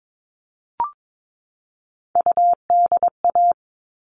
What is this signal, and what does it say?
Morse code, heard as a keyed tone, about 22 wpm, 700 Hz. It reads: UDA
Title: All Cut Numbers - 3-Digit 22wpm